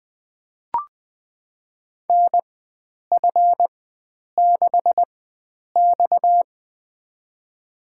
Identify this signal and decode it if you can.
Morse code, heard as a keyed tone, about 20 wpm, spaced as for 9 wpm, 700 Hz. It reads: NF6X
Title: USA Call Signs - 4 Characters - 4x Character Spacing - 20wpm